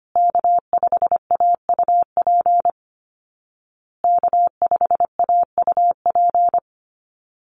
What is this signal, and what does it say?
Morse code, heard as a keyed tone, about 25 wpm, 700 Hz. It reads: K5AUP K5AUP
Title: Call Signs Repeated 2x - 25wpm